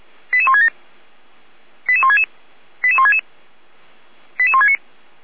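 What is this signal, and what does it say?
Five-tone selective call (ZVEI), heard as quick runs of five tones, tones of about 70 ms. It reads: 88167 88166 88166 88169
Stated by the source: ZVEI five-tone selective call received on an SDR.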